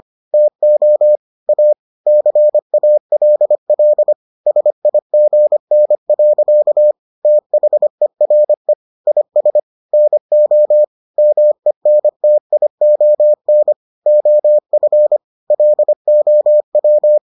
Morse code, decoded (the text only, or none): TO A CALL SIGN. THERE IS NO MENTION OF LOW